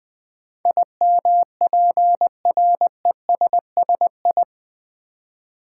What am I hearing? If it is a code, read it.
Morse code, heard as a keyed tone, about 20 wpm, 700 Hz. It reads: IMPRESSI
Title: Build A Word - 4 Letters At A Time - 10-Letter Words - 20wpm